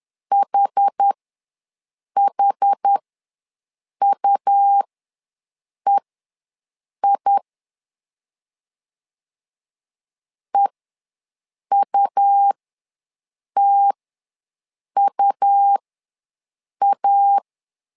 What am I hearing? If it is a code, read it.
Morse code, heard as a keyed tone, about 10 wpm, spaced as for 6 wpm, 800 Hz. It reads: HHUEI EUTUA
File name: LEZIONI / 2 UHDN / 2 UH ws 10wpm 2029